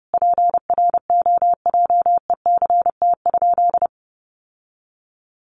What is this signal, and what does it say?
Morse code, heard as a keyed tone, about 30 wpm, 700 Hz. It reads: PROJECT?
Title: Sentences from Top 1400 Words 30wpm